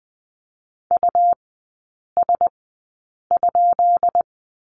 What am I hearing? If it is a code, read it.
Morse code, heard as a keyed tone, about 20 wpm, spaced as for 8 wpm, 700 Hz. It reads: US?
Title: 212 Lesson 34e - Number 8 - Three Characters 20wpm